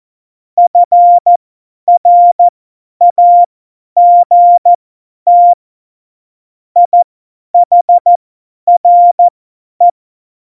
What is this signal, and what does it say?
Morse code, heard as a keyed tone, about 14 wpm, spaced as for 10 wpm, 700 Hz. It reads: FRAGT IHRE
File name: Morsekurs / Witze / Witz 26. September 17, Tempo 50